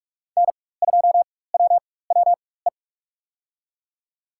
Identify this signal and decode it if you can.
Morse code, heard as a keyed tone, about 46 wpm, spaced as for 21 wpm, 700 Hz. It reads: N2WWE
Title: USA Call Signs - 5 Characters - 4x Character Spacing - 45wpm